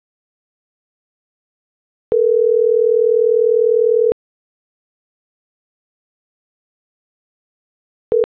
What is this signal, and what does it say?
Ringback tone (North America), 440 + 480 Hz.